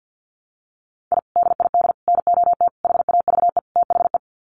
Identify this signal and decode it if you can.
Morse code, heard as a keyed tone, about 50 wpm, 700 Hz. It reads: I DID NOT HAVE THE
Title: Sets of 5 Words Encore 50wpm